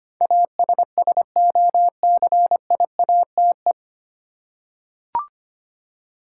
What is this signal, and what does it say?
Morse code, heard as a keyed tone, about 25 wpm, 700 Hz. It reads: ASSOCIATE
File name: practice / 25wpm / 9 Letter Words - First 3 Letters Given - 25wpm